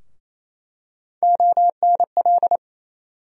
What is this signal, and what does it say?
Morse code, heard as a keyed tone, about 28 wpm, 700 Hz. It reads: ONL
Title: USA Call Sign Postfixes - 3 Letters 28wpm